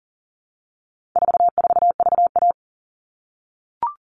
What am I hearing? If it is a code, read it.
Morse code, heard as a keyed tone, about 40 wpm, 700 Hz. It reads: 44VA
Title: All Cut Numbers - 4-Digit 40wpm